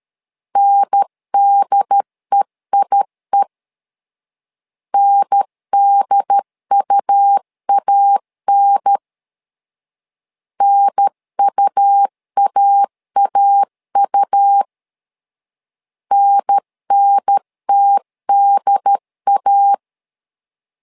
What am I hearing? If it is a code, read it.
Morse code, heard as a keyed tone, about 13 wpm, 800 Hz. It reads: NDEIE NDUAN NUAAU NNTDA
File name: LEZIONI / 2 UHDN / 2 UHDN 12wpm 1456